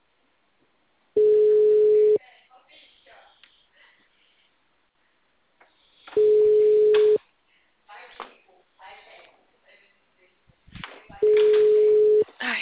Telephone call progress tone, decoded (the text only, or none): ringback tone